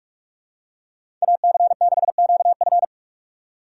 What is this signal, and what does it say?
Morse code, heard as a keyed tone, about 45 wpm, 700 Hz. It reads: AC6QF